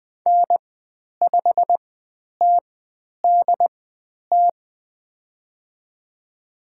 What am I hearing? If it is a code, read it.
Morse code, heard as a keyed tone, about 20 wpm, spaced as for 10 wpm, 700 Hz. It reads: N5TDT
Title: 095 Lesson 16g - Letter c - Call Signs 20wpm